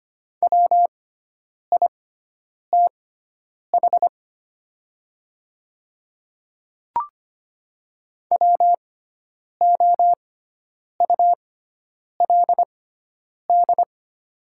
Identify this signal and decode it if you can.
Morse code, heard as a keyed tone, about 25 wpm, spaced as for 9 wpm, 700 Hz. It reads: WITH WOULD
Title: 6x Character Spacing - Top 300 Words - 25wpm